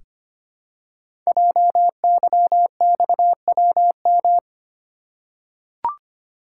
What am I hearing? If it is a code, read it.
Morse code, heard as a keyed tone, about 26 wpm, 700 Hz. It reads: JYXWM
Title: Code Groups - 5 Letters - 25wpm